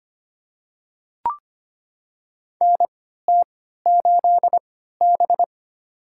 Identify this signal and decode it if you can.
Morse code, heard as a keyed tone, about 25 wpm, spaced as for 14 wpm, 700 Hz. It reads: NT8B
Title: USA Call Signs - 4 Characters - 3x Character Spacing - 25wpm